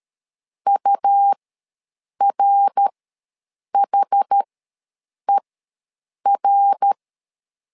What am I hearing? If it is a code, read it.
Morse code, heard as a keyed tone, about 13 wpm, spaced as for 7 wpm, 800 Hz. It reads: URHER
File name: LEZIONI / 3 MWRO / 3 MWR ws 12wpm 1809